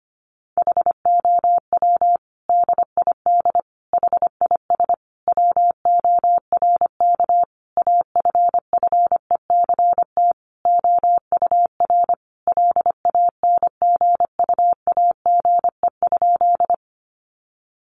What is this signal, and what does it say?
Morse code, heard as a keyed tone, about 25 wpm, 700 Hz. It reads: HOW DID HIS WORK AFFECT OUR LANGUAGE?
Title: Sentences from Top 900 Words 25wpm